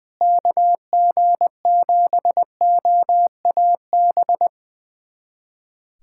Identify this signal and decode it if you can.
Morse code, heard as a keyed tone, about 20 wpm, 700 Hz. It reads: KG7OAB
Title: USA Callsign DE Callsign - 6 Characters - 20wpm